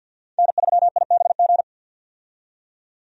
Morse code, heard as a keyed tone, about 50 wpm, 700 Hz. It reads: N3IBG
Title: USA Call Signs - 5 Characters - 50wpm